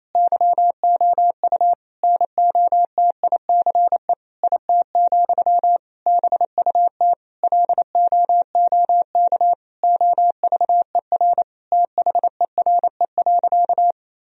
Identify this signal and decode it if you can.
Morse code, heard as a keyed tone, about 27 wpm, 700 Hz. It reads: YOU NOTICE IT, BUT LOOK OVER THERE.
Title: Sentences from Top 1000 Words - Review Part 2 - 28wpm